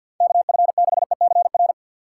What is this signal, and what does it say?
Morse code, heard as a keyed tone, about 52 wpm, 700 Hz. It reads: KF6ECR
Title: USA Call Signs - 6 Characters - 3x Character Spacing - 50wpm